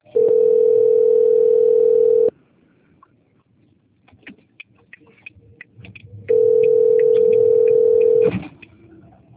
Ringback tone (North America), 440 + 480 Hz.